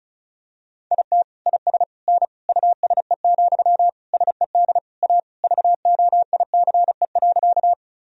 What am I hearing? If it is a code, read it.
Morse code, heard as a keyed tone, about 36 wpm, 700 Hz. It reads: IT IS N USE, SED A VOICE.
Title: Sentences from Top 500 Words and Abbreviations 35wpm